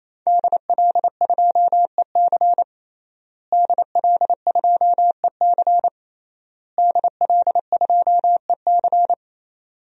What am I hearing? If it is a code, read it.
Morse code, heard as a keyed tone, about 27 wpm, 700 Hz. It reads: DL2EC DL2EC DL2EC
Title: DXCC-100 Call Signs - 5 Characters - Repeated 3x - 28wpm